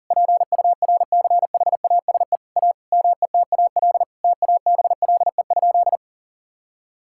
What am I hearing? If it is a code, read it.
Morse code, heard as a keyed tone, about 41 wpm, 700 Hz. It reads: PURCHASE A METAL TABLE?
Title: Sentences from Top 1000 Words - Review Part 6 - 40wpm